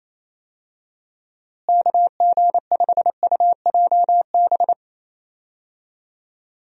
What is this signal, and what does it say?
Morse code, heard as a keyed tone, about 28 wpm, 700 Hz. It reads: KG5UJB
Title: USA Call Signs - 6 Characters - Repeated 3x - 28wpm